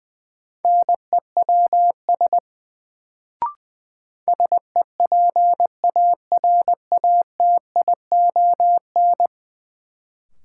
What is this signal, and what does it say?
Morse code, heard as a keyed tone, about 20 wpm, 700 Hz. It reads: NEWS SEPARATION
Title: Sets of 5 Words - New Word Drill - 20wpm